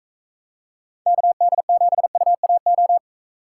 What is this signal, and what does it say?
Morse code, heard as a keyed tone, about 44 wpm, 700 Hz. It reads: KD7UAO